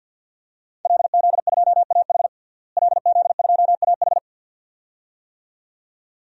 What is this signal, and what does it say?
Morse code, heard as a keyed tone, about 51 wpm, 700 Hz. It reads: LZ2AH LZ2AH
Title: DXCC-100 Call Signs - 5 Characters - Repeated 2x - 50wpm